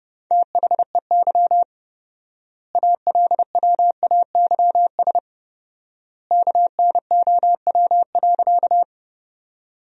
Morse code, heard as a keyed tone, about 30 wpm, 700 Hz. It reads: THEY ALWAYS KNOW.
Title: 4x Word-Spacing - Sentences from Top 300 Words - 30wpm